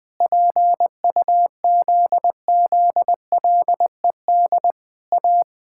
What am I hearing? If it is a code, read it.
Morse code, heard as a keyed tone, about 20 wpm, 700 Hz. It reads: PUZZLED A
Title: Pangram Challenge v2 - 20wpm